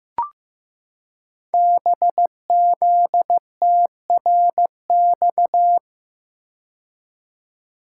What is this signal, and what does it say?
Morse code, heard as a keyed tone, about 15 wpm, 700 Hz. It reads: BZTRX
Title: Code Groups - 5 Letter:Numbers - 15wpm